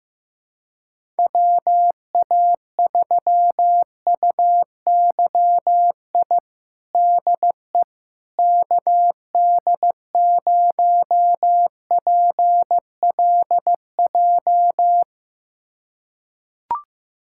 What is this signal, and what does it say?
Morse code, heard as a keyed tone, about 15 wpm, 700 Hz. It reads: WA3UYI DE KD0PLJ
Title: USA Callsign DE Callsign - 6 Characters - 15wpm